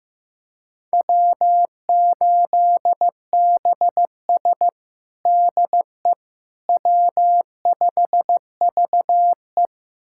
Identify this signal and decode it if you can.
Morse code, heard as a keyed tone, about 15 wpm, 700 Hz. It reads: W8BS DE W5VE
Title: USA Callsign DE Callsign - 4 Characters - 15wpm